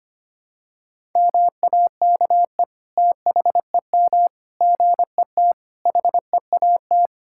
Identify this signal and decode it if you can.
Morse code, heard as a keyed tone, about 25 wpm, 700 Hz. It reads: MAKE THEM GET HEAT
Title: Sentences from Top 1600 Words 25wpm